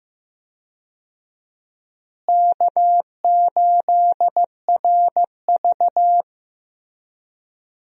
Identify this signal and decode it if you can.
Morse code, heard as a keyed tone, about 15 wpm, 700 Hz. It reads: K8RV
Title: USA Call Signs - 4 Characters - Repeated 2x - 15wpm